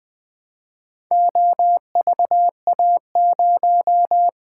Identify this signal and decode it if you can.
Morse code, heard as a keyed tone, about 20 wpm, 700 Hz. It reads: OVA0
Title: Four Letter-Number Combinations - 20wpm